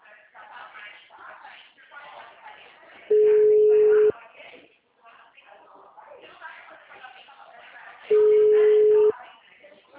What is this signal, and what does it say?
Ringback tone (Europe, 425 Hz), with speech.